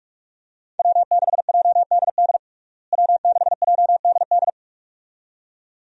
Morse code, heard as a keyed tone, about 48 wpm, 700 Hz. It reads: W6JDD W6JDD
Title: USA Call Signs - 5 Characters - Repeated 2x - 45wpm